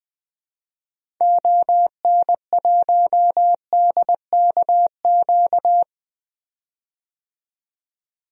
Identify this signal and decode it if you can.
Morse code, heard as a keyed tone, about 20 wpm, 700 Hz. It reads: ON1DKQ
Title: DXCC-100 Call Signs - 6 Characters - 20wpm